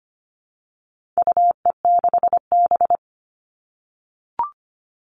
Morse code, heard as a keyed tone, about 25 wpm, 700 Hz. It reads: UE6B
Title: All Cut Numbers - 4-Digit 25wpm